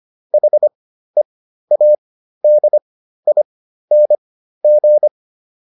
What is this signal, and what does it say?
Morse code, heard as a keyed tone, about 25 wpm, spaced as for 13 wpm, 600 Hz. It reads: HEADING